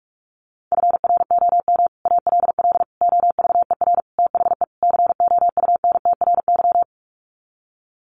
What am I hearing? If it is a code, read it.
Morse code, heard as a keyed tone, about 44 wpm, 700 Hz. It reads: FROM ALL OVER THE COUNTRY
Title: Sets of 5 Words Encore 45wpm